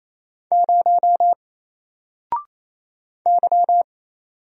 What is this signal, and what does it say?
Morse code, heard as a keyed tone, about 29 wpm, 700 Hz. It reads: 0 Y
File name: Single-Letter-Number-28wpm